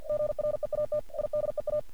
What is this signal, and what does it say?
Morse code, heard as a keyed tone, about 46 wpm, 600 Hz. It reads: GREAT IDEA
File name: archived / CRSnet